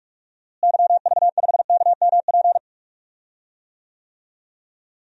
Morse code, heard as a keyed tone, about 46 wpm, 700 Hz. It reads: YV5KMP
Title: DXCC-100 Call Signs - 6 Characters - Repeated 2x - 45wpm